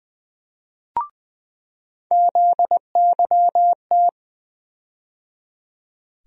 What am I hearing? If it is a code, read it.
Morse code, heard as a keyed tone, about 21 wpm, 700 Hz. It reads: ZYT
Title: USA Call Sign Postfixes - 3 Letters 20wpm